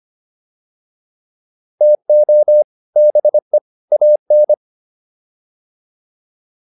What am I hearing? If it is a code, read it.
Morse code, heard as a keyed tone, about 25 wpm, 600 Hz. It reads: TO BE AN
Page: intermediate-practice-files